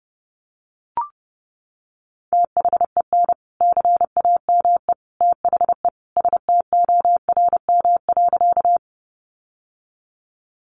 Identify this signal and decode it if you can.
Morse code, heard as a keyed tone, about 28 wpm, 700 Hz. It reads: THEN CAME THE STORM.
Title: Sentences from Top 1500 Words 30wpm